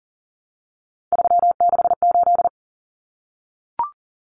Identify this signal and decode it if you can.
Morse code, heard as a keyed tone, about 40 wpm, 700 Hz. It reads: 368
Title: Numbers - 3-Digit 40wpm